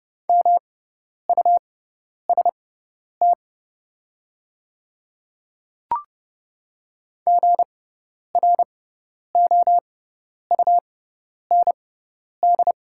Morse code, heard as a keyed tone, about 30 wpm, spaced as for 10 wpm, 700 Hz. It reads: MUST GROUND
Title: 6x Character Spacing - Top 500 Words - 30wpm